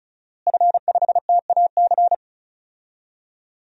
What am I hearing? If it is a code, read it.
Morse code, heard as a keyed tone, about 34 wpm, 700 Hz. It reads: F5TAC
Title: DXCC-100 Call Signs - 5 Characters - Repeated 2x - 35wpm